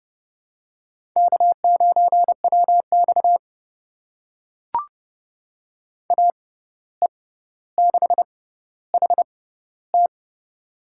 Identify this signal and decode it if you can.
Morse code, heard as a keyed tone, about 31 wpm, spaced as for 15 wpm, 700 Hz. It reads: K9WX AE6HT
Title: USA Call Signs - 6x Character Spacing - 30wpm